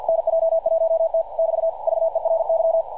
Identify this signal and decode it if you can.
Morse code, heard as a keyed tone, about 52 wpm, 590 Hz, with noise. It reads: NJ1T <BT> VERY
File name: archived / CRSnet